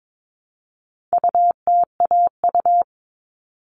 Morse code, heard as a keyed tone, about 22 wpm, 700 Hz. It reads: UTAU
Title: All Cut Numbers - 4-Digit 22wpm